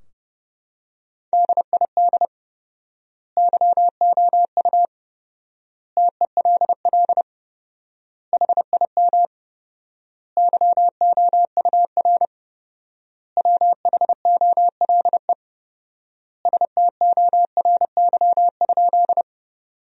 Morse code, heard as a keyed tone, about 32 wpm, 700 Hz. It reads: DID YOU TELL HIM YOUR WHOLE STORY?
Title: 4x Word-Spacing - Sentences from Top 400 Words - 30wpm